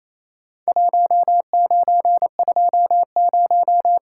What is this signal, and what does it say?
Morse code, heard as a keyed tone, about 28 wpm, 700 Hz. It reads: 1920
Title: Worked All Europe DX Contest - QTC Exchange - 28wpm